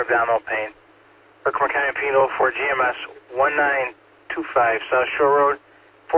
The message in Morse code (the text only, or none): none